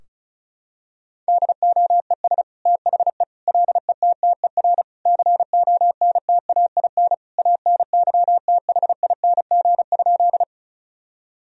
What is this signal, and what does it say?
Morse code, heard as a keyed tone, about 34 wpm, 700 Hz. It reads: DOES THE LETTER CONTAIN ANYTHING?
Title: Sentences from Top 1000 Words - Review Part 2 - 35wpm